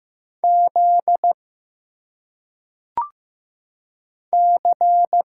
Morse code, heard as a keyed tone, about 15 wpm, 700 Hz. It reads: Z C
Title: Single Letters - Phonetic Pronunciation 15wpm